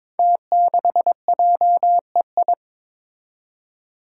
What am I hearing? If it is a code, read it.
Morse code, heard as a keyed tone, about 22 wpm, 700 Hz. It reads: T6JEI
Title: Five Challenging Letter-Number Combinations 22wpm